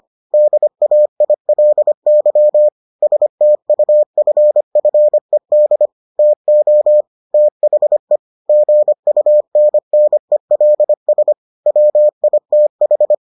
Morse code, heard as a keyed tone, about 25 wpm, 600 Hz. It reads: DAILY STUFFED TO THE GUNNELS WITH